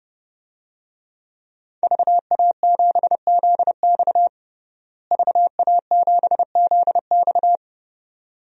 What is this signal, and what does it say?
Morse code, heard as a keyed tone, about 32 wpm, 700 Hz. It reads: VA7ZX VA7ZX
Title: DXCC-100 Call Signs - 5 Characters - Repeated 2x - 30wpm